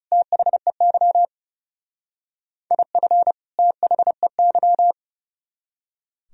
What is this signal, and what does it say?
Morse code, heard as a keyed tone, about 38 wpm, 700 Hz. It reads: THEY IF THEY